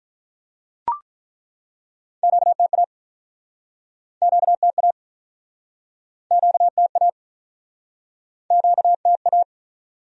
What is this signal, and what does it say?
Morse code, heard as a keyed tone, about 48 wpm, 700 Hz. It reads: QTA QTA QTA QTA